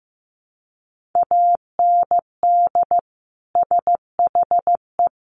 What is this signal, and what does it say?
Morse code, heard as a keyed tone, about 15 wpm, 700 Hz. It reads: AND SHE
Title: Sets of 2 Words 15wpm